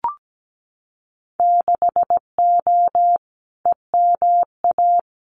Morse code, heard as a keyed tone, about 17 wpm, 700 Hz. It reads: 6O EMA
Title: Winter Field Day Exchanges Encore 17wpm